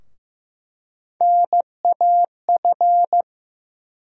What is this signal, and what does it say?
Morse code, heard as a keyed tone, about 15 wpm, 700 Hz. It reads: NAF
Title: USA Call Sign Postfixes - 3 Letters 15wpm